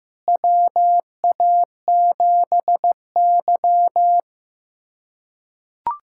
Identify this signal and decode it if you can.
Morse code, heard as a keyed tone, about 15 wpm, 700 Hz. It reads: WA7Y
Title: USA Callsign DE Callsign - 4 Characters - 15wpm